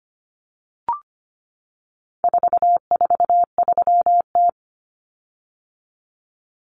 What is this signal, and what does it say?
Morse code, heard as a keyed tone, about 25 wpm, 700 Hz. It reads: 443T
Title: Common Cut Numbers - 4-Digit 25wpm